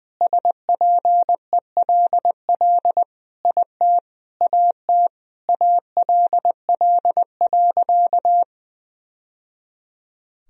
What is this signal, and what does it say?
Morse code, heard as a keyed tone, about 21 wpm, 700 Hz. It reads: SPELL IT AT ALL.
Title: Sentences from Top 1000 Words - Review Part 1 - 20wpm